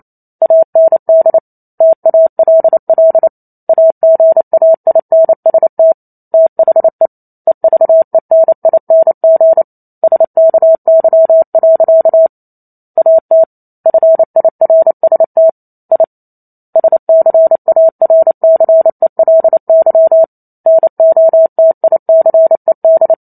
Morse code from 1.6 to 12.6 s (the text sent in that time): TALL AGAINST THE EVENING SKY.